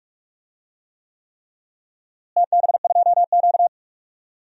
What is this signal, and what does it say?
Morse code, heard as a keyed tone, about 47 wpm, 700 Hz. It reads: TB2Q
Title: DXCC-100 Call Signs - 4 Characters - Repeated 3x - 45wpm